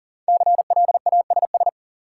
Morse code, heard as a keyed tone, about 40 wpm, 700 Hz. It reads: CLASS
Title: Sentences from Top 1000 Words - Review Part 2 - New Word Drill - 40wpm